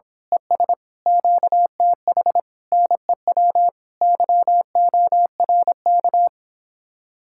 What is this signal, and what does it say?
Morse code, heard as a keyed tone, about 26 wpm, 700 Hz. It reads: ES QTH NEW YORK
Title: Speed-Racing Common QSO Phrases - 22wpm